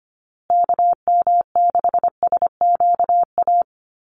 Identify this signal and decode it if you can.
Morse code, heard as a keyed tone, about 25 wpm, 700 Hz. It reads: KM6SQA
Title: USA Call Signs Repeated 2x 25wpm